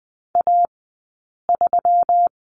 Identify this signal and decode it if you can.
Morse code, heard as a keyed tone, about 20 wpm, spaced as for 8 wpm, 700 Hz. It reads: A3
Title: 171 Lesson 28d - Number 9 - Two Characters 20wpm